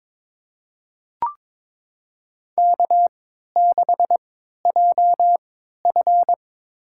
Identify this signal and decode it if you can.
Morse code, heard as a keyed tone, about 22 wpm, spaced as for 13 wpm, 700 Hz. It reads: K6JF
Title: USA Call Signs - 4 Characters - 3x Character Spacing - 22wpm